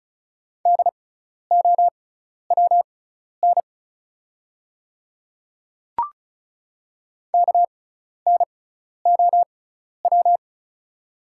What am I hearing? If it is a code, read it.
Morse code, heard as a keyed tone, about 36 wpm, spaced as for 12 wpm, 700 Hz. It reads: DOWN KNOW
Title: 6x Character Spacing - Top 100 Words - 35wpm